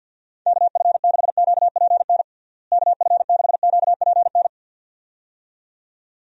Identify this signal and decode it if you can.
Morse code, heard as a keyed tone, about 50 wpm, 700 Hz. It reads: KF6QPN KF6QPN
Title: USA Call Signs - 6 Characters - Repeated 2x - 50wpm